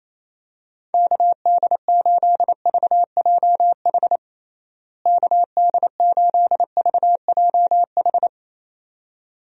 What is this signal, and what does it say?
Morse code, heard as a keyed tone, about 27 wpm, 700 Hz. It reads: KD8VJH KD8VJH
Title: USA Call Signs - 6 Characters - Repeated 2x - 28wpm